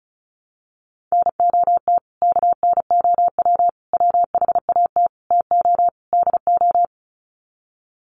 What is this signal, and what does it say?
Morse code, heard as a keyed tone, about 34 wpm, 700 Hz. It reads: NOT KNOW WHAT TO DO
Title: Sets of 5 Words 35wpm